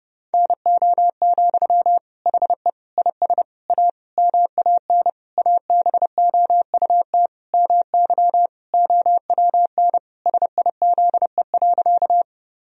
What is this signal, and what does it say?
Morse code, heard as a keyed tone, about 30 wpm, 700 Hz. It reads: NO, HE IS A MAN ABOUT MY OWN SIZE.
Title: Sentences from Top 1000 Words - Review Part 2 - 30wpm